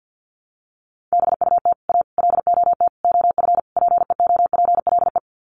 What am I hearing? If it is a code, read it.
Morse code, heard as a keyed tone, about 50 wpm, 700 Hz. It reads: BUT A LOT OF PEOPLE
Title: Sets of 5 Words Encore2 50wpm